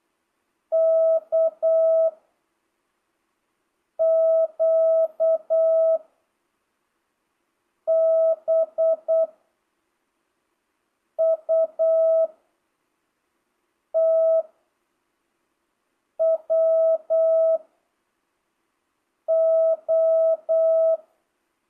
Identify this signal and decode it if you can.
Morse code, heard as a keyed tone, about 8 wpm, 660 Hz. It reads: K Q B U T W O
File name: jh8cbh / 2025cw